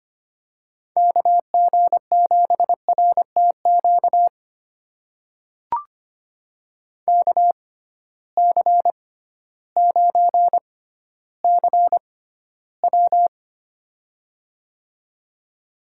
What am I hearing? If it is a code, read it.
Morse code, heard as a keyed tone, about 25 wpm, spaced as for 14 wpm, 700 Hz. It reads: KG7RTQ KC9CW
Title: USA Call Signs - 6x Character Spacing - 25wpm